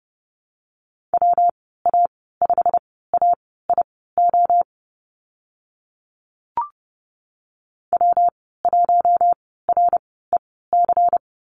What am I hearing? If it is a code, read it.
Morse code, heard as a keyed tone, about 30 wpm, spaced as for 17 wpm, 700 Hz. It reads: WA5AIO W1REC
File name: Lesson 19g - Letter w - Call Signs 30wpm